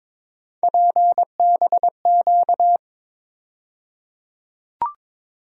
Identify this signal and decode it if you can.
Morse code, heard as a keyed tone, about 22 wpm, 700 Hz. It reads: PBQ
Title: USA Call Sign Postfixes - 3 Letters 22wpm